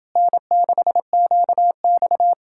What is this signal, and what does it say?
Morse code, heard as a keyed tone, about 29 wpm, 700 Hz. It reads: N6QX